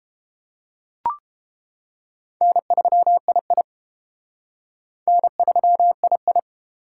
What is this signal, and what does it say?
Morse code, heard as a keyed tone, about 30 wpm, 700 Hz. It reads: N3II N3II